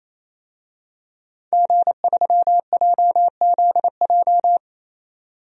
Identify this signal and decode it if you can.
Morse code, heard as a keyed tone, about 28 wpm, 700 Hz. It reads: G3JZJ
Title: DXCC-100 Call Signs - 5 Characters - 28wpm